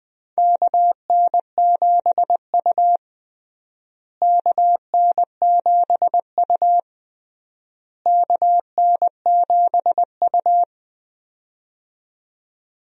Morse code, heard as a keyed tone, about 20 wpm, 700 Hz. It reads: KN7U KN7U KN7U
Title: USA Call Signs - 4 Characters - Repeated 3x - 20wpm